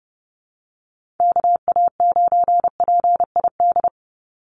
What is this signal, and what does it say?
Morse code, heard as a keyed tone, about 30 wpm, 700 Hz. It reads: KA9PID
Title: USA Call Signs 30wpm